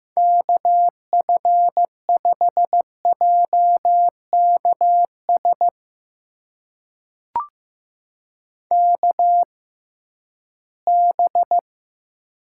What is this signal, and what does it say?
Morse code, heard as a keyed tone, about 15 wpm, spaced as for 11 wpm, 700 Hz. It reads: KF5JKS KB
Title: USA Call Signs - 6 Characters - 6x Character Spacing - 15wpm